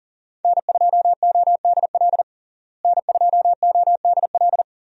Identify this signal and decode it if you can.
Morse code, heard as a keyed tone, about 37 wpm, 700 Hz. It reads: N2ODL N2ODL
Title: USA Call Signs - 5 Characters - Repeated 2x - 40wpm